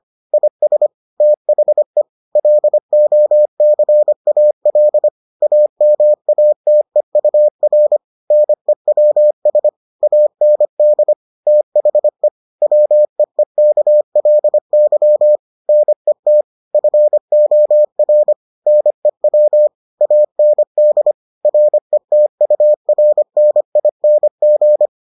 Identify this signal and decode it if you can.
Morse code, heard as a keyed tone, about 25 wpm, 600 Hz. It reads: IS THE LOCAL AMATEUR NEWS AND THE WEEKLY NET FOR NEW AND RETURNING